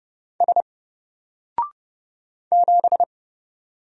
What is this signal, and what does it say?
Morse code, heard as a keyed tone, about 30 wpm, 700 Hz. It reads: S 7
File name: Single-Letter-Number-30wpm